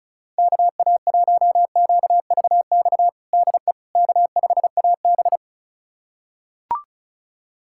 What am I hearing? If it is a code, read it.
Morse code, heard as a keyed tone, about 36 wpm, 700 Hz. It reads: KA1QVX DE K5AB
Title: USA Callsign DE Callsign 35wpm